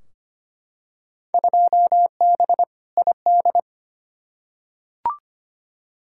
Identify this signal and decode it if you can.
Morse code, heard as a keyed tone, about 25 wpm, 700 Hz. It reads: 2B ID
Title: Field Day Exchanges Encore - 25wpm